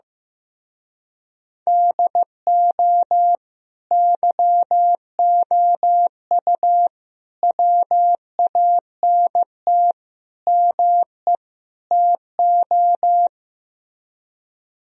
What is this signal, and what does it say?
Morse code, heard as a keyed tone, about 15 wpm, 700 Hz. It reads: DO YOU WANT ME TO
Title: Speed-Racing - Sets of 5 Words 15wpm